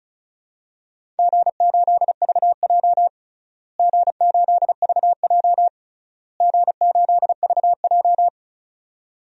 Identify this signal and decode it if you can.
Morse code, heard as a keyed tone, about 36 wpm, 700 Hz. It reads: G8VJ G8VJ G8VJ
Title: DXCC-100 Call Signs - 4 Characters - Repeated 3x - 35wpm